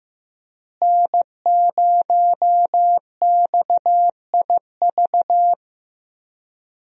Morse code, heard as a keyed tone, about 15 wpm, 700 Hz. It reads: N0XIV
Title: USA Call Signs - 5 Characters - Repeated 3x - 15wpm